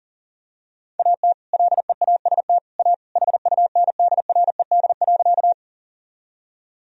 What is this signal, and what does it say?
Morse code, heard as a keyed tone, about 40 wpm, 700 Hz. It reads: AT LEAST A HUNDRED.
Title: Sentences from Top 1000 Words - Review Part 2 - 40wpm